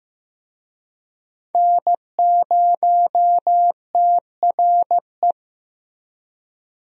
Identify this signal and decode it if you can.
Morse code, heard as a keyed tone, about 15 wpm, 700 Hz. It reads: N0TRE
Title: USA Call Signs - 5 Characters - Repeated 2x - 15wpm